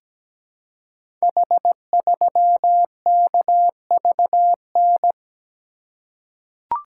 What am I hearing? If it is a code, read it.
Morse code, heard as a keyed tone, about 17 wpm, 700 Hz. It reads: H3KVN
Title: Code Groups - 5 Letter:Numbers - 17wpm